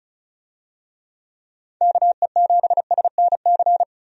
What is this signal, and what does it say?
Morse code, heard as a keyed tone, about 36 wpm, 700 Hz. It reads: KE7SNC
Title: USA Call Signs - 6 Characters - 35wpm